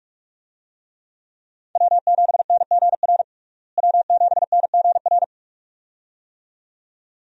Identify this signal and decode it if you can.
Morse code, heard as a keyed tone, about 47 wpm, 700 Hz. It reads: W7NGR W7NGR
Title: USA Call Signs - 5 Characters - Repeated 2x - 45wpm